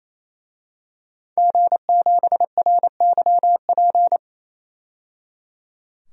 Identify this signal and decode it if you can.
Morse code, heard as a keyed tone, about 28 wpm, 700 Hz. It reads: G7RYP
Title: DXCC-100 Call Signs - 5 Characters - 28wpm